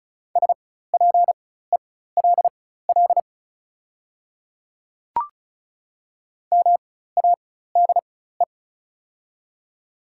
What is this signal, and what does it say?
Morse code, heard as a keyed tone, about 34 wpm, spaced as for 16 wpm, 690 Hz. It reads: SPELL MADE
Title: 4x Character Spacing - Top 200 Words - 35wpm